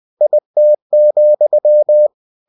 Morse code, heard as a keyed tone, about 20 wpm, 600 Hz. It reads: IT,
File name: cw-books / winnie-the-pooh / winnie-20wpm-page-0021